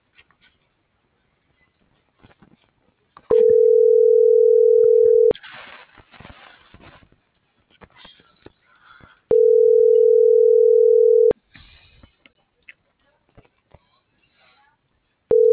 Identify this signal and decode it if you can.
Ringback tone (North America), 440 + 480 Hz.